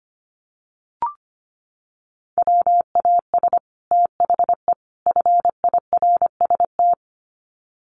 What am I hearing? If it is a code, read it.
Morse code, heard as a keyed tone, about 25 wpm, 700 Hz. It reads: WAS THE FIRST
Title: Sets of 3 Words 25wpm